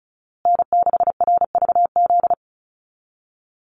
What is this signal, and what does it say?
Morse code, heard as a keyed tone, about 34 wpm, 700 Hz. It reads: N6RVZ
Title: USA Call Signs Repeated 3x 35wpm